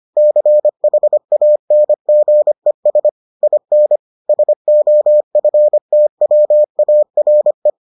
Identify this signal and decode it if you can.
Morse code, heard as a keyed tone, about 25 wpm, 600 Hz. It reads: CHANGES IN SOFTWARE